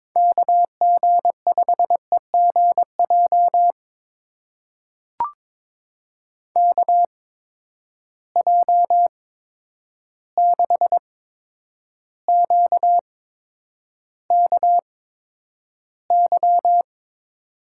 Morse code, heard as a keyed tone, about 21 wpm, spaced as for 9 wpm, 700 Hz. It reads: KG5EGJ KJ6QKY
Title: USA Call Signs - 6 Characters - 8x Character Spacing - 22wpm